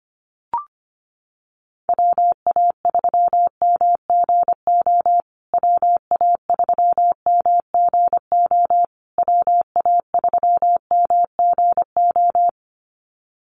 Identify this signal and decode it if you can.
Morse code, heard as a keyed tone, about 25 wpm, 700 Hz. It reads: WA3MGO WA3MGO WA3MGO
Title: USA Call Signs Repeated 3x 25wpm